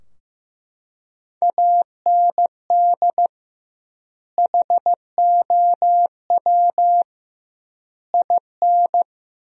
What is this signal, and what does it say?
Morse code, heard as a keyed tone, about 16 wpm, 700 Hz. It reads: AND HOW IN
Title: 2x Word-Spacing - Sentences from Top 200 Words - 15wpm